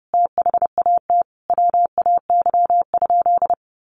Morse code, heard as a keyed tone, about 30 wpm, 700 Hz. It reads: THAT WAY?
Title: Sentences from Top 1100 Words 30wpm